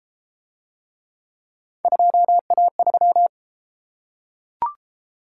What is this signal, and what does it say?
Morse code, heard as a keyed tone, about 33 wpm, 700 Hz. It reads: 2A3